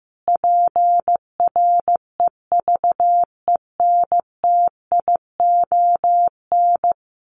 Morse code, heard as a keyed tone, about 15 wpm, 700 Hz. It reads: PREVENTION
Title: Sets of 5 Words Encore2 15wpm